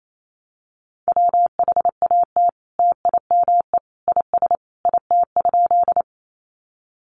Morse code, heard as a keyed tone, about 28 wpm, 700 Hz. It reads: WHAT TIME IS IT?
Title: Sentences from Top 100 Words 28wpm